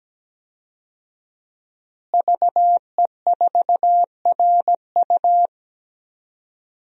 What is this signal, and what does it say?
Morse code, heard as a keyed tone, about 17 wpm, 700 Hz. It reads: VE4RU
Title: DXCC-100 Call Signs - 5 Characters - Repeated 3x - 17wpm